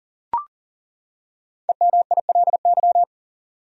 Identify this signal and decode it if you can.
Morse code, heard as a keyed tone, about 40 wpm, 700 Hz. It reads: EMILY
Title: USA Common Names - Max 5 Letters - 40wpm